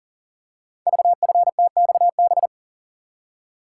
Morse code, heard as a keyed tone, about 39 wpm, 700 Hz. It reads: VFTXB
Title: Code Groups - 5 Letters - 40wpm